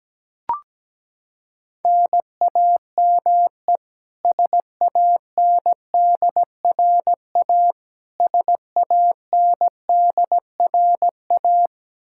Morse code, heard as a keyed tone, about 17 wpm, 700 Hz. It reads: NAME SANDRA SANDRA
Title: Practice QSO - Exchange 1 - Beginning - 17wpm